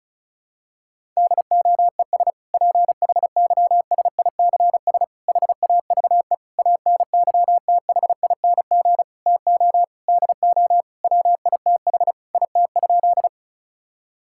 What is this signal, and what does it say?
Morse code, heard as a keyed tone, about 36 wpm, 700 Hz. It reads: DOES PHYSICS HAVE ANYTHING TO DO WITH IT?
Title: Sentences from Top 1000 Words - Review Part 5 - 35wpm